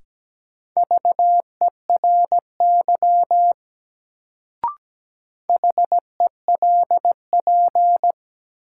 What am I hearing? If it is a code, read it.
Morse code, heard as a keyed tone, about 17 wpm, 700 Hz. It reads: VERY HELP
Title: Sentences from Top 200 Words - New Word Drill - 17wpm